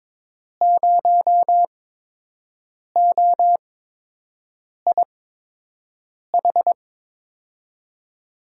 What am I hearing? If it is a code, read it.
Morse code, heard as a keyed tone, about 22 wpm, spaced as for 6 wpm, 700 Hz. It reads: 0OIH
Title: USA Call Signs - 5 Characters - 8x Character Spacing - 22wpm